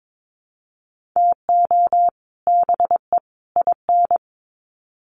Morse code, heard as a keyed tone, about 22 wpm, 700 Hz. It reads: TO BE IN
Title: Sets of 3 Words 22wpm